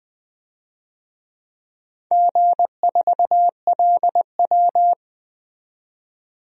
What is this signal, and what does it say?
Morse code, heard as a keyed tone, about 20 wpm, 700 Hz. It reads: G4LW
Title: DXCC-100 Call Signs - 4 Characters - 20wpm